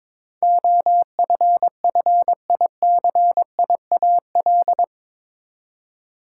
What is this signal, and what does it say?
Morse code, heard as a keyed tone, about 22 wpm, 700 Hz. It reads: OFFICIAL
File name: practice / 22wpm / Eight Letter Words - 22wpm